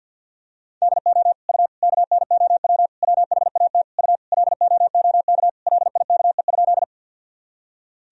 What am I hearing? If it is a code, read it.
Morse code, heard as a keyed tone, about 50 wpm, 700 Hz. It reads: DO U KNOW WHAT U LOOK LIKE?
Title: Sentences from Top 100 Words and Abbreviations 50wpm